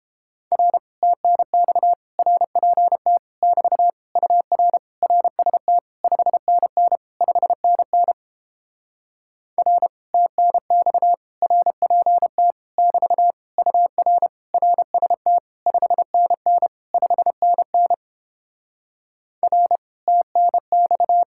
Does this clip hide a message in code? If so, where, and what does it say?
Yes: Morse code from 0.5 to 21.2 s, heard as a keyed tone, about 31 wpm, 700 Hz. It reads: R TNX RPT <BT> UR RST 5NN 5NN R TNX RPT <BT> UR RST 5NN 5NN R TNX